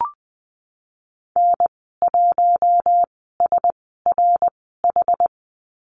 Morse code, heard as a keyed tone, about 20 wpm, spaced as for 15 wpm, 700 Hz. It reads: N1SRH
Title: 173 Lesson 28g - Number 9 - Call Signs 20wpm